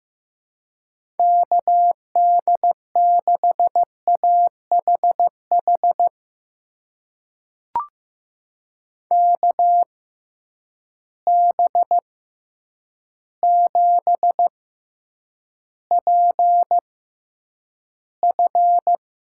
Morse code, heard as a keyed tone, about 15 wpm, spaced as for 8 wpm, 700 Hz. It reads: KD6AHH KB7PF
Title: USA Call Signs - 6 Characters - 6x Character Spacing - 15wpm